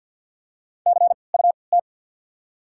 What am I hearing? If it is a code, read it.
Morse code, heard as a keyed tone, about 53 wpm, spaced as for 29 wpm, 690 Hz. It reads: CUT
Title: 3x Character Spacing - Top 500 Words - 50wpm